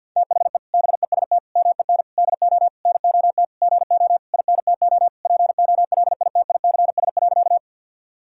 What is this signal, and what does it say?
Morse code, heard as a keyed tone, about 51 wpm, 700 Hz. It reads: THE BEST MEN DO NOT GO INTO POLITICS.
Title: Sentences from Top 1000 Words - Review Part 6 - 50wpm